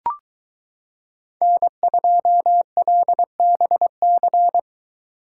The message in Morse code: N2LBC

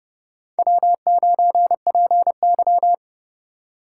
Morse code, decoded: W9PY

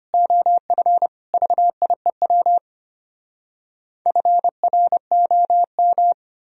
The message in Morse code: OF VIEW FROM

